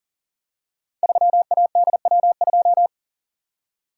Morse code, heard as a keyed tone, about 43 wpm, 700 Hz. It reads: 3ADW2
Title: Code Groups - 5 Letter:Numbers - 40wpm